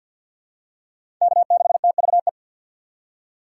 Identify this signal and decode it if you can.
Morse code, heard as a keyed tone, about 50 wpm, 700 Hz. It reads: K6TVE
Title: USA Call Signs - 3x Character Spacing - 50wpm